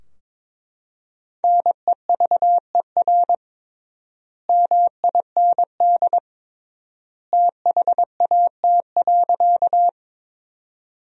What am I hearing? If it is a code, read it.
Morse code, heard as a keyed tone, about 21 wpm, 700 Hz. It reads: NEVER MIND THAT.
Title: 3x Word-Spacing - Sentences from Top 500 Words - 22wpm